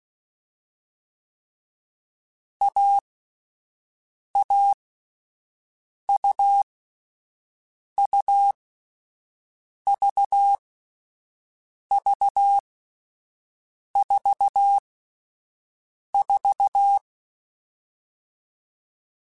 Morse code, heard as a keyed tone, about 16 wpm, 800 Hz. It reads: A A U U V V 4 4